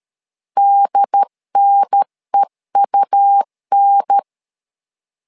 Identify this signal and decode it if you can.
Morse code, heard as a keyed tone, about 13 wpm, 800 Hz. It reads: DNEUN
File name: LEZIONI / 2 UHDN / 2 UHDN 12wpm 1456